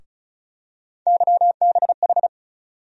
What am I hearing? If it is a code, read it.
Morse code, heard as a keyed tone, about 36 wpm, 700 Hz. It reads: YBH
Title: USA Call Sign Postfixes - 3 Letters 35wpm